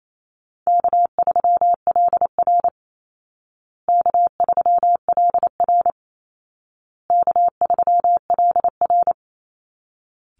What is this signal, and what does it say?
Morse code, heard as a keyed tone, about 28 wpm, 700 Hz. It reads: K3LR K3LR K3LR
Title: Easy Call Signs Repeated 3x - 28wpm